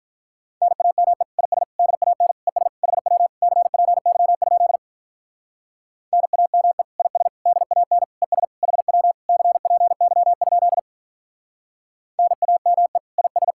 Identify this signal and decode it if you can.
Morse code, heard as a keyed tone, about 44 wpm, 700 Hz. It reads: NAME IS DAN ES HW CPY? NAME IS DAN ES HW CPY? NAME IS
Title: Speed-Racing Common QSO Phrases - 35wpm